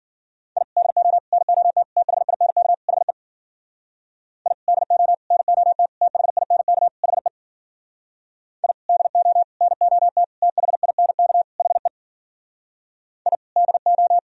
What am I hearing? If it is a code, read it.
Morse code, heard as a keyed tone, about 52 wpm, 700 Hz. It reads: I DO NOT THINK HE I DO NOT THINK HE I DO NOT THINK HE I DO